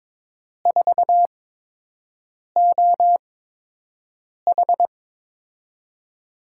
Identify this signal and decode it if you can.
Morse code, heard as a keyed tone, about 21 wpm, spaced as for 6 wpm, 700 Hz. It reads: 4OH
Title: USA Call Signs - 8x Character Spacing - 22wpm